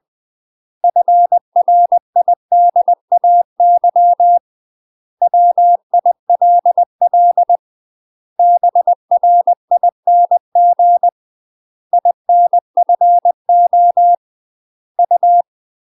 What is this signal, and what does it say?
Morse code, heard as a keyed tone, about 20 wpm, 700 Hz. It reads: FRIDAY WILL BRING INFO U
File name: UK MASTER QSOs TRAINING / 20WPM / UK MASTER QSOs 0062